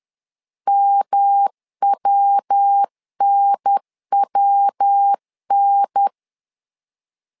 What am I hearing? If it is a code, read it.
Morse code, heard as a keyed tone, about 10 wpm, 800 Hz. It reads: MWNWN
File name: LEZIONI / 3 MWRO / 3 MW 10wpm 1959